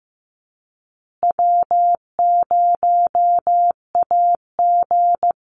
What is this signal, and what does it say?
Morse code, heard as a keyed tone, about 15 wpm, 700 Hz. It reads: W0AG
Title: Call Signs Repeated 2x - 15wpm